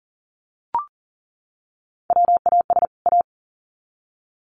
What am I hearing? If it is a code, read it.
Morse code, heard as a keyed tone, about 40 wpm, 700 Hz. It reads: WAS A
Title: Sets of 2 Words 40wpm